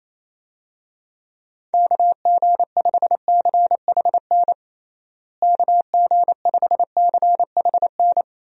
Morse code, heard as a keyed tone, about 28 wpm, 700 Hz. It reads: KG5CHN KG5CHN
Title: USA Call Signs - 6 Characters - Repeated 3x - 28wpm